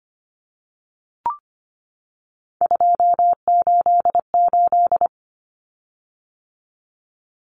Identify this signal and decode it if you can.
Morse code, heard as a keyed tone, about 25 wpm, 700 Hz. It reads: 288
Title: Common Cut Numbers - 3-Digit 25wpm